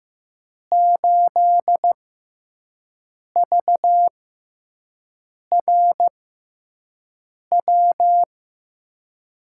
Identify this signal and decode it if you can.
Morse code, heard as a keyed tone, about 15 wpm, spaced as for 5 wpm, 700 Hz. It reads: 8VRW
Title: USA Call Signs - 6 Characters - 6x Character Spacing - 15wpm